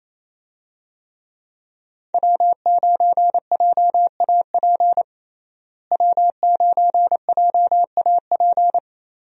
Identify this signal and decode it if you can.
Morse code, heard as a keyed tone, about 28 wpm, 700 Hz. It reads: W9JAP W9JAP
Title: USA Call Signs - 5 Characters - Repeated 3x - 28wpm